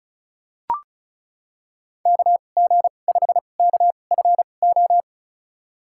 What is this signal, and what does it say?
Morse code, heard as a keyed tone, about 36 wpm, spaced as for 25 wpm, 700 Hz. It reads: KG5KFO
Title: USA Call Signs - 6 Characters - 2x Character Spacing - 35wpm